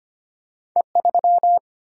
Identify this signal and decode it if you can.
Morse code, heard as a keyed tone, about 25 wpm, 700 Hz. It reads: E3
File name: Two Letter-Number Combinations - 25wpm